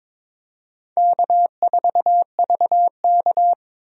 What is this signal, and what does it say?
Morse code, heard as a keyed tone, about 22 wpm, 700 Hz. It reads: K4VK
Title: USA Call Signs - 4 Characters - 22wpm